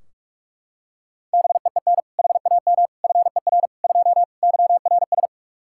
Morse code, heard as a keyed tone, about 44 wpm, 700 Hz. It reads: BEEN HAM FER 2 YRS
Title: K1USN SST QSO Practice 45wpm